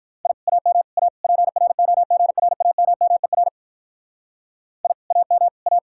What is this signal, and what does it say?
Morse code, heard as a keyed tone, about 43 wpm, 700 Hz. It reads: I AM A PROGRAMMER I AM A
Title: Speed-Racing Common QSO Phrases Encore - 35wpm